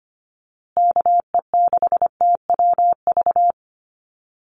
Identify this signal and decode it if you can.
Morse code, heard as a keyed tone, about 25 wpm, 700 Hz. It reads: KE6TWV
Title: USA Call Signs Repeated 2x 25wpm